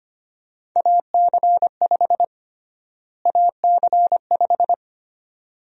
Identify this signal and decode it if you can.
Morse code, heard as a keyed tone, about 25 wpm, 700 Hz. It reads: AC5 AC5
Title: USA Call Sign Prefixes - Repeated 2x - 25wpm